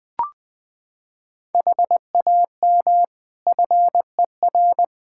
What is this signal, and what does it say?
Morse code, heard as a keyed tone, about 21 wpm, 700 Hz. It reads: HAM FER
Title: Practice QSO - Exchange 3 - More Info - 20wpm